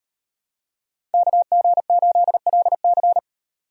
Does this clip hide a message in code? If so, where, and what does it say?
Yes: Morse code from 1.1 to 3.2 s, heard as a keyed tone, about 38 wpm, 700 Hz. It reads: KG8LC